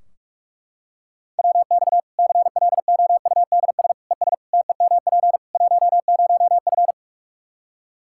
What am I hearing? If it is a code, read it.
Morse code, heard as a keyed tone, about 41 wpm, 700 Hz. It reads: WX CLOUDS ES TEMP 10F
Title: Practice QSO - Exchange 2 - Info - 45wpm